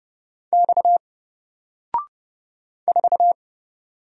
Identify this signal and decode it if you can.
Morse code, heard as a keyed tone, about 29 wpm, 700 Hz. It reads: X 4
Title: Single Character (Letter:Number:Punctuation:Prosign) 30wpm